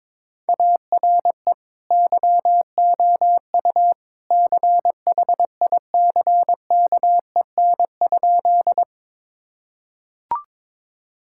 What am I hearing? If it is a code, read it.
Morse code, heard as a keyed tone, about 22 wpm, 700 Hz. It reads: ARE YOU CHICKEN?
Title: Sentences from Top 1000 Words - Review Part 5 - 22wpm